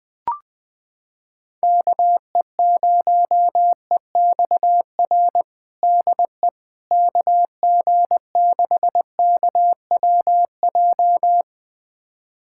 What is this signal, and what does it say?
Morse code, heard as a keyed tone, about 20 wpm, 700 Hz. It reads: KE0EXR DE KG6KWJ
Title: USA Callsign DE Callsign - 6 Characters - 20wpm